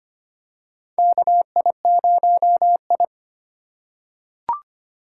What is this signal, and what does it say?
Morse code, heard as a keyed tone, about 25 wpm, 700 Hz. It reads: KI0I
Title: K1USN SST QSO Practice - 25wpm